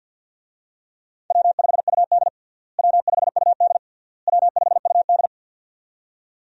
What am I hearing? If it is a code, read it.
Morse code, heard as a keyed tone, about 49 wpm, 700 Hz. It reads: W5UD W5UD W5UD
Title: USA Call Signs - 4 Characters - Repeated 3x - 50wpm